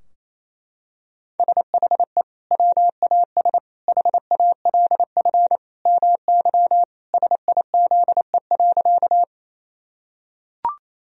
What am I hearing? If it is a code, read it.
Morse code, heard as a keyed tone, about 28 wpm, 700 Hz. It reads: SHE WAS HALF MY SIZE.
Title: Sentences from Top 1000 Words - Review Part 2 - 28wpm